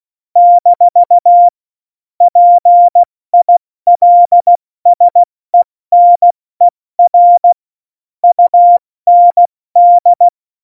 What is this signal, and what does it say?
Morse code, heard as a keyed tone, about 16 wpm, 700 Hz. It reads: - PILSENER UND
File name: Morsekurs / Witze / Witz 17. November 2020, 70 F80